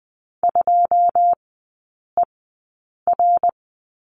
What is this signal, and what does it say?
Morse code, heard as a keyed tone, about 20 wpm, spaced as for 9 wpm, 690 Hz. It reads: 2ER
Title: 081 Lesson 14e - Number 2 - Three Characters 20wpm